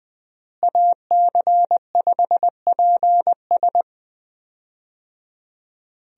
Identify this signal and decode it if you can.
Morse code, heard as a keyed tone, about 20 wpm, 700 Hz. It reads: AC5PS
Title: USA Call Signs - 5 Characters - Repeated 3x - 20wpm